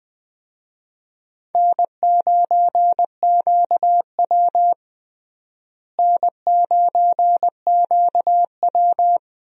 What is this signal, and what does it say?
Morse code, heard as a keyed tone, about 20 wpm, 700 Hz. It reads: N9QW N9QW
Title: USA Call Signs - 4 Characters - Repeated 2x - 20wpm